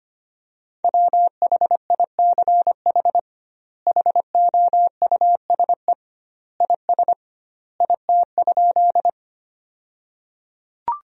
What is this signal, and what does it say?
Morse code, heard as a keyed tone, about 25 wpm, 700 Hz. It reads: WHICH HOUSE IS IT?
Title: 2x Word-Spacing - Sentences from Top 200 Words - 25wpm